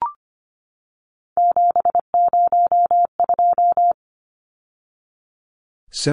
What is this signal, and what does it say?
Morse code, heard as a keyed tone, about 25 wpm, 700 Hz. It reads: 702
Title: Numbers - 3-Digit 25wpm